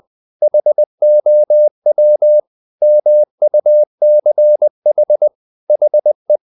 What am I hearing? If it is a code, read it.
Morse code, heard as a keyed tone, about 20 wpm, 600 Hz. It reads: HOW MUCH HE
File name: the old man and the sea / the old man and the sea part 1 20wpm